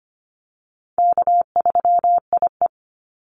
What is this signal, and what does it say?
Morse code, heard as a keyed tone, about 25 wpm, 700 Hz. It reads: K3IE
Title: Easy Call Signs Repeated 2x - 25wpm